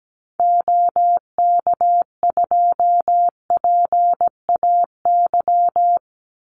Morse code, heard as a keyed tone, about 17 wpm, 700 Hz. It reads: OK2PAY
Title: Call Signs - 17wpm